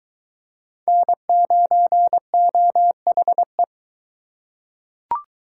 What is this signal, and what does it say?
Morse code, heard as a keyed tone, about 22 wpm, 700 Hz. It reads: N9OHE